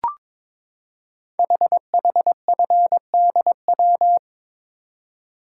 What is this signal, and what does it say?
Morse code, heard as a keyed tone, about 22 wpm, 700 Hz. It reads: HHFDW
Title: Code Groups - 5 Letters - 22wpm